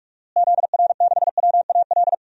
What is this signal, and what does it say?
Morse code, heard as a keyed tone, about 41 wpm, 700 Hz. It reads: ZR6WAL
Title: DXCC-100 Call Signs - 6 Characters - Repeated 3x - 45wpm